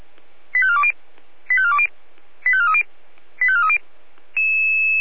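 Five-tone selective call (ZVEI), heard as quick runs of five tones, tones of about 70 ms. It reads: 85420 85420 85430 85430
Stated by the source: ZVEI five-tone selective call received by SDR.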